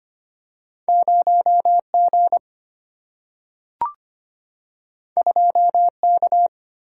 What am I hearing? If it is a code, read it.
Morse code, heard as a keyed tone, about 25 wpm, 700 Hz. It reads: 0 G 2 K
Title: Single-Letter Single-Number 25wpm